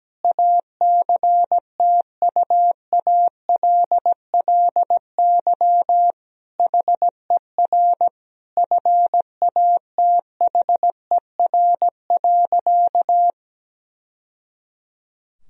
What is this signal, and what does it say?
Morse code, heard as a keyed tone, about 17 wpm, 700 Hz. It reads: ACTUALLY HER FATHER.
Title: Sentences from Top 1000 Words - Review Part 4 - 17wpm